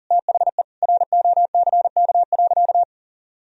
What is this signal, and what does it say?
Morse code, heard as a keyed tone, about 40 wpm, 700 Hz. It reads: THE ROCK.
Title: Sentences from Top 1000 Words - Review Part 2 - 40wpm